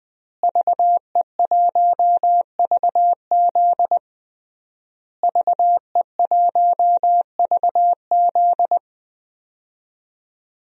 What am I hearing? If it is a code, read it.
Morse code, heard as a keyed tone, about 20 wpm, 700 Hz. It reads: VE1VZ VE1VZ
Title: DXCC-100 Call Signs - 5 Characters - Repeated 3x - 20wpm